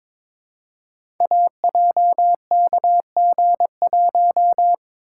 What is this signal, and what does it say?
Morse code, heard as a keyed tone, about 22 wpm, 700 Hz. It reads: AJKG1
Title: Code Groups - 5 Letter:Numbers - 22wpm